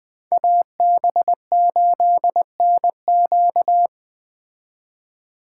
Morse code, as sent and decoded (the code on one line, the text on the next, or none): .- -... ---.. -. --.-
AB8NQ